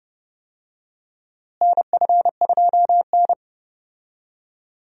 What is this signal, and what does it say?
Morse code, heard as a keyed tone, about 32 wpm, 700 Hz. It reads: NF2N
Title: USA Call Signs - 4 Characters - Repeated 3x - 30wpm